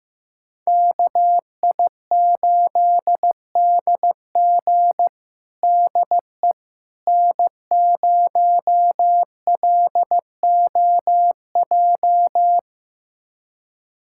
Morse code, heard as a keyed tone, about 15 wpm, 700 Hz. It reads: KI8DG DE N0LOJ
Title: USA Callsign DE Callsign - 5 Characters - 15wpm